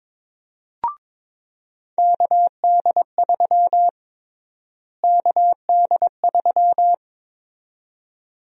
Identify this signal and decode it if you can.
Morse code, heard as a keyed tone, about 22 wpm, 700 Hz. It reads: KD3 KD3
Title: USA Call Sign Prefixes - Repeated 2x - 22wpm